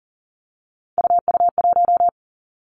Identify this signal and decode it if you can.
Morse code, heard as a keyed tone, about 38 wpm, 700 Hz. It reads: UU1
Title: Three Challenging Letter-Number Combinations 40wpm